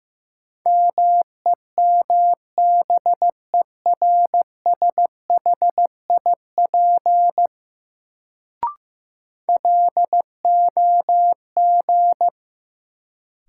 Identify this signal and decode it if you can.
Morse code, heard as a keyed tone, about 15 wpm, 700 Hz. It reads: MEMBERSHIP LOG
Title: Sentences from Top 1400 Words - New Word Drill - 15wpm